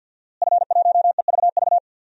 Morse code, heard as a keyed tone, about 51 wpm, 700 Hz. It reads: F1EVV
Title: DXCC-100 Call Signs - 5 Characters - Repeated 2x - 50wpm